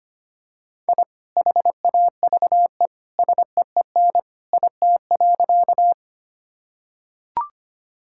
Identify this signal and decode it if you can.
Morse code, heard as a keyed tone, about 25 wpm, 700 Hz. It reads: I HAVE SEEN IT.
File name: practice / 25wpm / Sentences with 2-Word Phrases - 25wpm